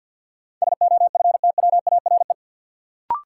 Morse code, heard as a keyed tone, about 51 wpm, 700 Hz. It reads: SOFTWARE
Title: Sentences from Top 1000 Words - Review Part 3 - New Word Drill - 50wpm